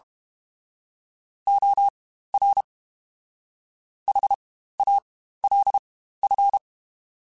Morse code, heard as a keyed tone, about 32 wpm, spaced as for 15 wpm, 800 Hz. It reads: OR HALF
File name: Spaced / Chuckles Spaced-mp3s / HalfFull sp @30